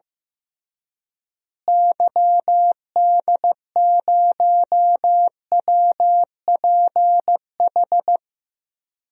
Morse code, heard as a keyed tone, about 15 wpm, 700 Hz. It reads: YD0WPH
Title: DXCC-100 Call Signs - 6 Characters - Repeated 3x - 15wpm